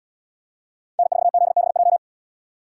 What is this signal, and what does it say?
Morse code, heard as a keyed tone, about 70 wpm, 700 Hz. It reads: N4ZBJ